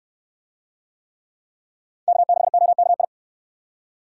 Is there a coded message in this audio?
Yes: Morse code from 2.1 to 3.0 s, heard as a keyed tone, about 68 wpm, 700 Hz. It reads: K6OKI